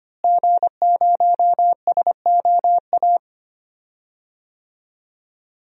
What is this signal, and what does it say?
Morse code, heard as a keyed tone, about 25 wpm, 700 Hz. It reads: G0SOA
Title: DXCC-100 Call Signs - 5 Characters - Repeated 3x - 25wpm